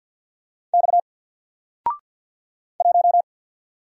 Morse code, heard as a keyed tone, about 48 wpm, 700 Hz. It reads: X 1
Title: Single Letter-Number Alt 50wpm